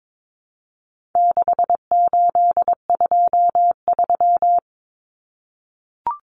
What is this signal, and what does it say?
Morse code, heard as a keyed tone, about 22 wpm, 700 Hz. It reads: 6823
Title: Numbers - 4-Digit 22wpm